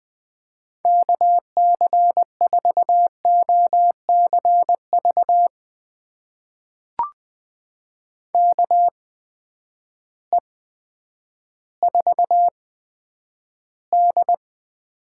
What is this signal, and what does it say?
Morse code, heard as a keyed tone, about 20 wpm, spaced as for 10 wpm, 700 Hz. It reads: KC4OCV KE4D
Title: USA Call Signs - 6 Characters - 8x Character Spacing - 20wpm